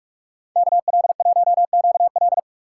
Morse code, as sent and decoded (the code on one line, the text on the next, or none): -.- .-.. .---- --.- .-..
KL1QL